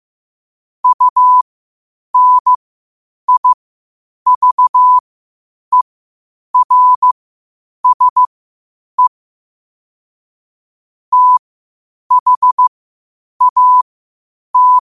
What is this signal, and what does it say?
Morse code, heard as a keyed tone, about 15 wpm, spaced as for 8 wpm, 1000 Hz. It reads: UNIVERSE THAT
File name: qotdfiles / 8wpm